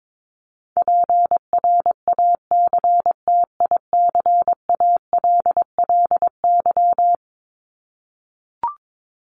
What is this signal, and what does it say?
Morse code, heard as a keyed tone, about 22 wpm, 700 Hz. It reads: PRACTICALLY
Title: LY Suffix - Top 100 Words with Base Words 22wpm